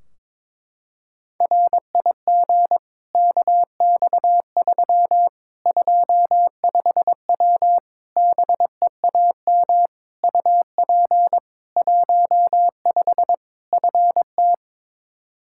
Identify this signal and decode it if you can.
Morse code, heard as a keyed tone, about 22 wpm, 700 Hz. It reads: RIG KX3 25W BEAM UP 15 FT
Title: K1USN SST QSO Practice 22wpm